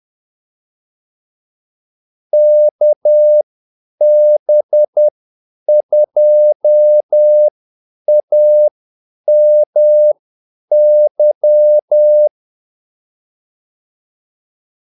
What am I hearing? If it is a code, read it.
Morse code, heard as a keyed tone, about 10 wpm, spaced as for 8 wpm, 600 Hz. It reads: KB2AMY
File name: Callsigns-10-8wpm